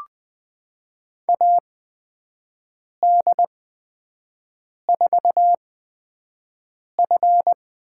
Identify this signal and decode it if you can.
Morse code, heard as a keyed tone, about 20 wpm, spaced as for 5 wpm, 700 Hz. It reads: AD4F
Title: USA Call Signs - 4 Characters - 8x Character Spacing - 20wpm